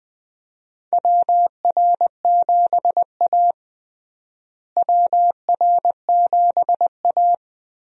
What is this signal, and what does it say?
Morse code, heard as a keyed tone, about 20 wpm, 700 Hz. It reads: WR7A WR7A
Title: USA Call Signs - 4 Characters - Repeated 2x - 20wpm